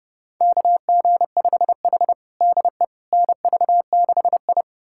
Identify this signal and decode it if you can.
Morse code, heard as a keyed tone, about 32 wpm, 700 Hz. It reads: KG5H DE NV6I
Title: USA Callsign DE Callsign - 4 Characters - 30wpm